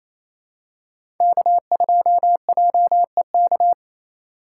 Morse code, heard as a keyed tone, about 28 wpm, 700 Hz. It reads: K2JEK
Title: USA Call Signs - 5 Characters - 4x Character Spacing - 28wpm